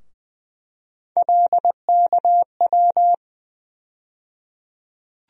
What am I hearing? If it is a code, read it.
Morse code, heard as a keyed tone, about 20 wpm, 700 Hz. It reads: LKW
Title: USA Call Sign Postfixes - 3 Letters 20wpm